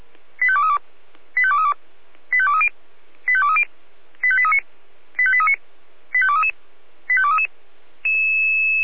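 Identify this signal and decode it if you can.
Five-tone selective call (ZVEI), heard as quick runs of five tones, tones of about 70 ms. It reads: 85232 85232 85230 85230 85839 85839 85233 85233
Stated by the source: ZVEI five-tone selective call received by SDR.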